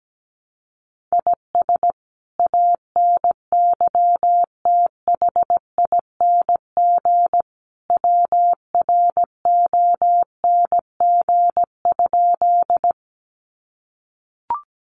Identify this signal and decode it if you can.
Morse code, heard as a keyed tone, about 17 wpm, 700 Hz. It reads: IS ANYTHING WRONG?
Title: Sentences from Top 1000 Words 17wpm